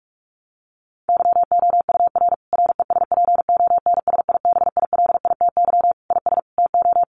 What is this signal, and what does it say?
Morse code, heard as a keyed tone, about 44 wpm, 700 Hz. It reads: YOUR RESPONSIBILITY IS TO
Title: Sentences from Top 1200 Words 45wpm